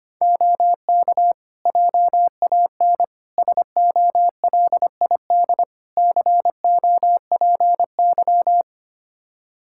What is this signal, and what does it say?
Morse code, heard as a keyed tone, about 25 wpm, 700 Hz. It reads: OK JAN SOLID COPY
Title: K1USN SST QSO Practice 25wpm